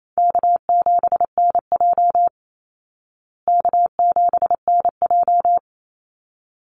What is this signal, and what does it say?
Morse code, heard as a keyed tone, about 28 wpm, 700 Hz. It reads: K7NJ K7NJ
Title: Easy Call Signs Repeated 2x - 28wpm